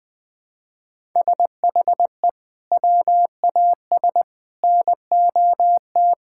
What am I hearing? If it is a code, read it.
Morse code, heard as a keyed tone, about 20 wpm, 700 Hz. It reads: SHE WAS NOT
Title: Sentences with 6-Word Phrases - 20wpm